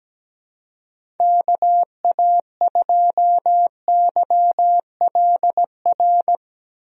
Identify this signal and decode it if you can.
Morse code, heard as a keyed tone, about 17 wpm, 700 Hz. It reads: KA2YLR
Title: USA Call Signs - 6 Characters - Repeated 3x - 17wpm